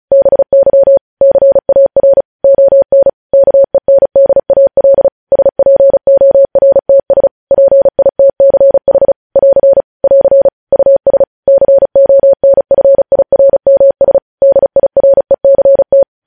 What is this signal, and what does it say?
Morse code, heard as a keyed tone, about 36 wpm, 550 Hz. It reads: BY CAR ON KENDAL SPORTS PITCH <AR> <AR> US CONFIRMS DIRECT
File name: podcast / News headlines-1x-35x35